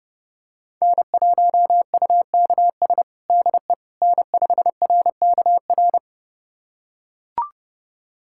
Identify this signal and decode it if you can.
Morse code, heard as a keyed tone, about 31 wpm, 700 Hz. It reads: N1UKS DE N5RKR
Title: USA Callsign DE Callsign - 5 Characters - 30wpm